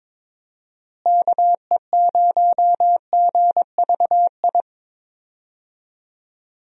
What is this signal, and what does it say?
Morse code, heard as a keyed tone, about 21 wpm, 700 Hz. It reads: KE0GVI
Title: USA Call Signs - 6 Characters - Repeated 2x - 22wpm